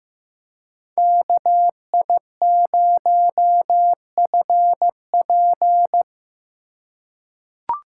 Morse code, heard as a keyed tone, about 15 wpm, 700 Hz. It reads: KI0FP
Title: USA Call Signs - 5 Characters - 4x Character Spacing - 15wpm